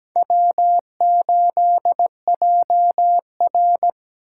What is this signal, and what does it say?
Morse code, heard as a keyed tone, about 20 wpm, 700 Hz. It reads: W8JR
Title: Speed Racing - USA Call Signs 17wpm